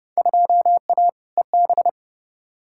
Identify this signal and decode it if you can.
Morse code, heard as a keyed tone, about 30 wpm, 700 Hz. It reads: 2A EB
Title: Field Day Exchanges - 30wpm